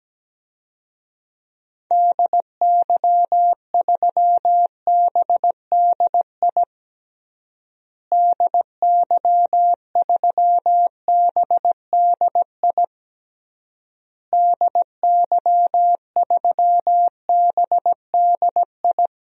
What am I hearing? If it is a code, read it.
Morse code, heard as a keyed tone, about 17 wpm, 700 Hz. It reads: DY3BDI DY3BDI DY3BDI
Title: DXCC-100 Call Signs - 6 Characters - Repeated 3x - 17wpm